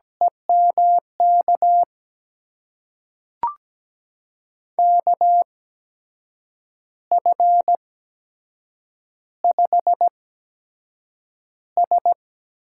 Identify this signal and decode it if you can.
Morse code, heard as a keyed tone, about 17 wpm, spaced as for 7 wpm, 700 Hz. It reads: EMK KF5S
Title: USA Call Signs - 6 Characters - 8x Character Spacing - 17wpm